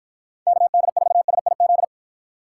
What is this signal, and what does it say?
Morse code, heard as a keyed tone, about 47 wpm, 700 Hz. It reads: KD4SIZ